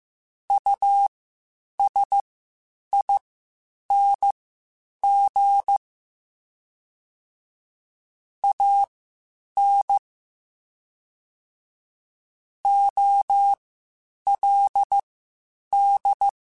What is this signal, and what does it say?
Morse code, heard as a keyed tone, about 15 wpm, spaced as for 8 wpm, 800 Hz. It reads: USING AN OLD
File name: (spaced) / QSOs Spaced-mp3s / WD0FBT 1 @13 sp